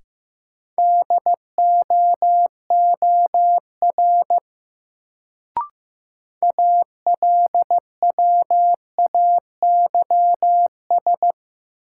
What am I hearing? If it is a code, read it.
Morse code, heard as a keyed tone, about 15 wpm, 700 Hz. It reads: DOOR ALWAYS
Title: Sentences from Top 300 Words - New Word Drill - 15wpm